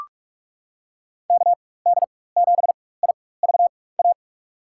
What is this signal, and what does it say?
Morse code, heard as a keyed tone, about 44 wpm, spaced as for 21 wpm, 700 Hz. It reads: KD7IVA
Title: USA Call Signs - 4x Character Spacing - 45wpm